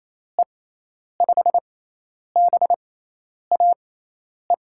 Morse code, heard as a keyed tone, about 28 wpm, spaced as for 10 wpm, 700 Hz. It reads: E5BAE
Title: USA Call Signs - 6 Characters - 6x Character Spacing - 28wpm